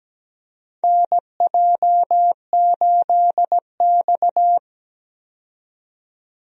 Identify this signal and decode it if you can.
Morse code, heard as a keyed tone, about 17 wpm, 700 Hz. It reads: NJ8X
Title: USA Call Signs - 4 Characters - 17wpm